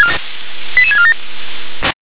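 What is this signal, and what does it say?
Five-tone selective call (ZVEI), heard as quick runs of five tones, tones of about 70 ms. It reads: 88647
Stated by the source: ZVEI five-tone selective call 88084 received by SDR.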